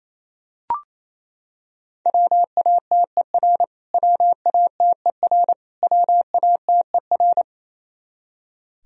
Morse code, heard as a keyed tone, about 28 wpm, 700 Hz. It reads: WATER WATER WATER
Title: Top 100 Words Repeated 28wpm